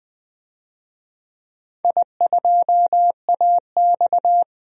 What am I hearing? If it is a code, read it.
Morse code, heard as a keyed tone, about 20 wpm, 700 Hz. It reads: I2AX
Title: DXCC-100 Call Signs - 4 Characters - 20wpm